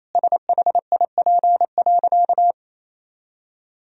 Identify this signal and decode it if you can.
Morse code, heard as a keyed tone, about 28 wpm, 700 Hz. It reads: SHIP.
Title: Sentences from Top 1000 Words - Review Part 2 - 28wpm